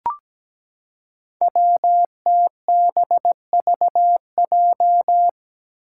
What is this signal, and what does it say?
Morse code, heard as a keyed tone, about 17 wpm, 700 Hz. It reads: WTBVJ
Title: Five Challenging Letter-Number Combinations 17wpm